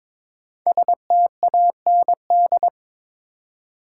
Morse code, heard as a keyed tone, about 22 wpm, 700 Hz. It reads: STAND
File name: practice / 22wpm / 5 Letter Words - First 3 Letters Given - 22wpm